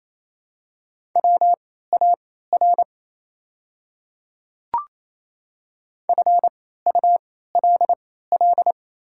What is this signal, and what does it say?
Morse code, heard as a keyed tone, about 28 wpm, spaced as for 16 wpm, 700 Hz. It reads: WAR FULL
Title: 3x Character Spacing - Top 500 Words - 28wpm